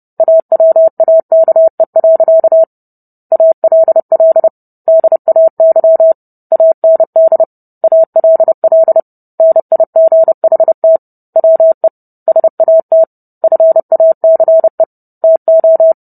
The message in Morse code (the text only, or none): AWAKE. ALL DAY AND ALL NIGHT WE SAT FACE TO